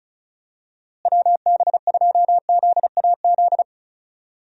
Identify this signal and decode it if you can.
Morse code, heard as a keyed tone, about 34 wpm, 700 Hz. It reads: WB2ZAZ
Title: USA Call Signs - 6 Characters - Repeated 3x - 35wpm